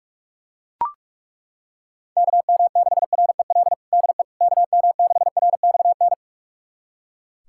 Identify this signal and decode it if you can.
Morse code, heard as a keyed tone, about 41 wpm, 700 Hz. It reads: KM6REL DE KM6RXN
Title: USA Callsign DE Callsign - 6 Characters - 45wpm